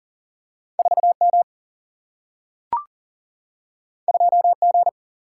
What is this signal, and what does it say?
Morse code, heard as a keyed tone, about 39 wpm, 700 Hz. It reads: 4 M 2 G
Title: Single-Letter Single-Number 40wpm